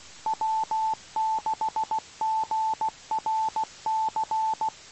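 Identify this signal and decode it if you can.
Morse code, heard as a keyed tone, about 16 wpm, 880 Hz. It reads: W6GRC
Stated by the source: Morse code received on the 2 m band at 147.105 MHz.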